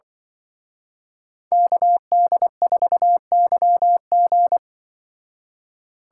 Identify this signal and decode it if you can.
Morse code, heard as a keyed tone, about 24 wpm, 700 Hz. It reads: KD4YG
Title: Speed Racing - USA 2x2 Call Signs 20wpm